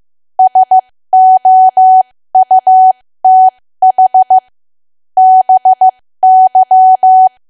Morse code, heard as a keyed tone, about 15 wpm, spaced as for 13 wpm, 750 Hz. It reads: SOUTH BY
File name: cwa / W1AW-Level2 / 13WPM